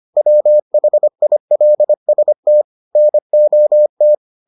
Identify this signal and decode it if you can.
Morse code, heard as a keyed tone, about 25 wpm, 600 Hz. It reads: WHILST NOT